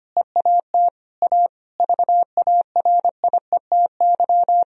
Morse code, heard as a keyed tone, about 25 wpm, 700 Hz. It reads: EAT A VARIETY
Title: Sentences from Top 1000 Words - Review Part 4 - 25wpm